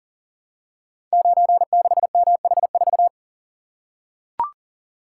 Morse code, heard as a keyed tone, about 42 wpm, 700 Hz. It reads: 96MH4
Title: Code Groups - 5 Letter:Numbers - 40wpm